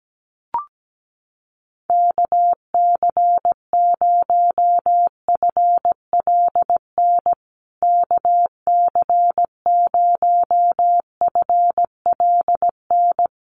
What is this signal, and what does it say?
Morse code, heard as a keyed tone, about 17 wpm, 700 Hz. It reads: KC0FLN KC0FLN
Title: USA Call Signs Repeated 3x 17wpm